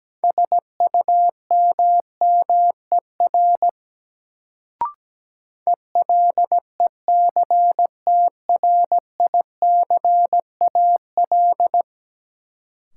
Morse code, heard as a keyed tone, about 17 wpm, 700 Hz. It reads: SUMMER ELECTRICAL
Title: Sentences from Top 1000 Words - Review Part 6 - New Word Drill - 17wpm